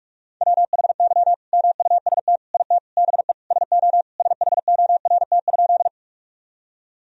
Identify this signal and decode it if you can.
Morse code, heard as a keyed tone, about 46 wpm, 700 Hz. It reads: WHY MUST IT BE SO SHORT?
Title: Sentences from Top 1000 Words - Review Part 2 - 45wpm